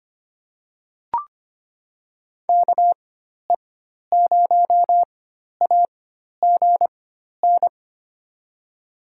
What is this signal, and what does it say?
Morse code, heard as a keyed tone, about 25 wpm, spaced as for 12 wpm, 700 Hz. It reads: KE0AGN
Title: USA Call Signs - 4x Character Spacing - 25wpm